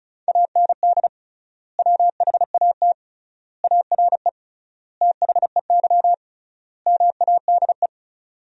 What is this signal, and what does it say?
Morse code, heard as a keyed tone, about 35 wpm, 700 Hz. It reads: AND WHAT ARE THEY MADE
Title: 3x Word-Spacing - Sentences from Top 200 Words - 35wpm